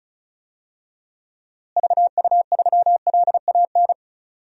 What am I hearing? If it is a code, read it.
Morse code, heard as a keyed tone, about 34 wpm, 700 Hz. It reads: VU3LAN
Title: DXCC-100 Call Signs - 6 Characters - Repeated 3x - 35wpm